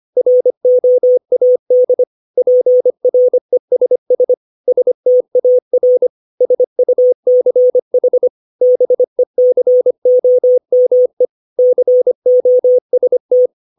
Morse code, heard as a keyed tone, about 25 wpm, 500 Hz. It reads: ROAD PRESS STAR SUCH BECOME COST
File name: 632 words / 632 Words 25WPM